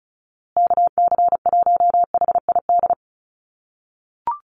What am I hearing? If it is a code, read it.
Morse code, heard as a keyed tone, about 34 wpm, 700 Hz. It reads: KC1HID
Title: USA Call Signs Repeated 3x 35wpm